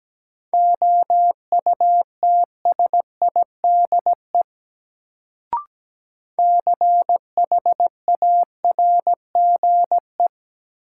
Morse code, heard as a keyed tone, about 17 wpm, 700 Hz. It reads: OUTSIDE CHARGE
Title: Sentences from Top 900 Words - New Word Drill - 17wpm